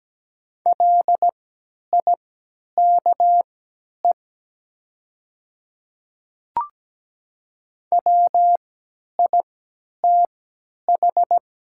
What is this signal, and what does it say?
Morse code, heard as a keyed tone, about 17 wpm, spaced as for 10 wpm, 700 Hz. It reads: LIKE WITH
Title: 3x Character Spacing - Top 200 Words - 17wpm